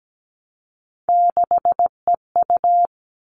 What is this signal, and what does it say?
Morse code, heard as a keyed tone, about 17 wpm, 700 Hz. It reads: 6EU
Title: All Cut Numbers - 3-Digit 17wpm